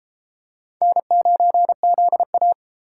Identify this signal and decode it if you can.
Morse code, heard as a keyed tone, about 33 wpm, 700 Hz. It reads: N9ZA